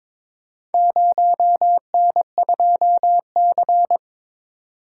Morse code, heard as a keyed tone, about 22 wpm, 700 Hz. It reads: 0N2C
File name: practice / 22wpm / Four Letter-Number Combinations - 22wpm